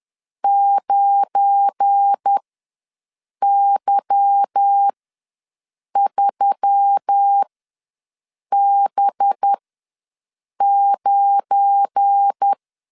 Morse code, heard as a keyed tone, about 10 wpm, spaced as for 6 wpm, 800 Hz. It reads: 9Y3B9